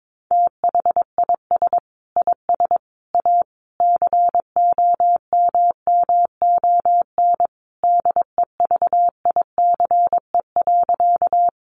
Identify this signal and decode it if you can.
Morse code, heard as a keyed tone, about 22 wpm, 700 Hz. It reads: THIS IS A COMMON DEVICE.
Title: Sentences from Top 1500 Words 22wpm